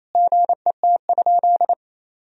Morse code, heard as a keyed tone, about 28 wpm, 700 Hz. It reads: GET?
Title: Sentences from Top 1000 Words - Review Part 3 - 28wpm